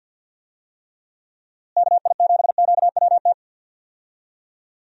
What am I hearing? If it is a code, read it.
Morse code, heard as a keyed tone, about 51 wpm, 700 Hz. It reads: KI7QWT
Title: USA Call Signs - 6 Characters - Repeated 2x - 50wpm